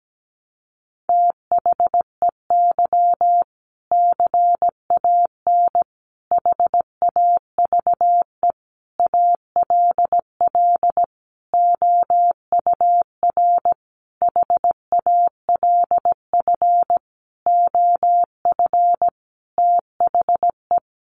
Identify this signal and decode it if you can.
Morse code, heard as a keyed tone, about 17 wpm, 700 Hz. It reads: THEY CAN HAVE ALL OUR HALF OF THE
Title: Sentences from Top 400 Words 17wpm